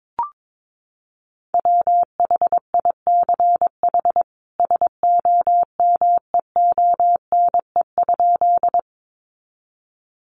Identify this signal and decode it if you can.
Morse code, heard as a keyed tone, about 22 wpm, 700 Hz. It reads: WHICH SOMEONE?
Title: Sentences from Top 600 Words 22wpm